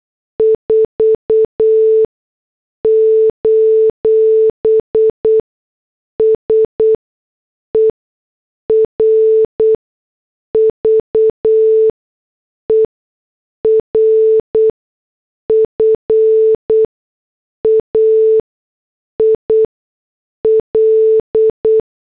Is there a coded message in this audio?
Yes: Morse code from 0.4 to 21.8 s, heard as a keyed tone, about 8 wpm, spaced as for 6 wpm, 440 Hz. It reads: 4: SERVER FAIL